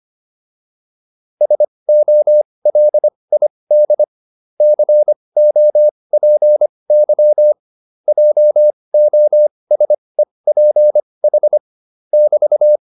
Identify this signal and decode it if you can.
Morse code, heard as a keyed tone, about 25 wpm, spaced as for 20 wpm, 600 Hz. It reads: SOLID COPY JOSEPH <BT>